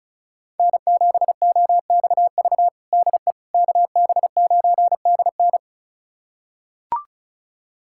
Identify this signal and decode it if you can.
Morse code, heard as a keyed tone, about 36 wpm, 700 Hz. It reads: N7OXV DE KB9DN
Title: USA Callsign DE Callsign - 5 Characters - 35wpm